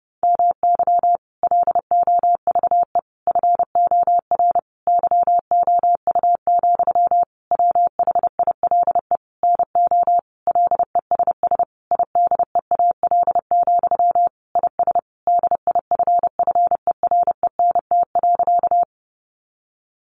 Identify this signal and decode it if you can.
Morse code, heard as a keyed tone, about 30 wpm, 700 Hz. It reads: MY LOVE FOR YOU, WHILE NO LESS IDEAL, IS DIFFERENT.
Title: Sentences from Top 1400 Words 30wpm